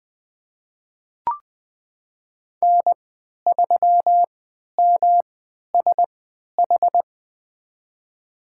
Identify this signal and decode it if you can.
Morse code, heard as a keyed tone, about 20 wpm, spaced as for 11 wpm, 700 Hz. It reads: N3MSH
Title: USA Call Signs - 3x Character Spacing - 20wpm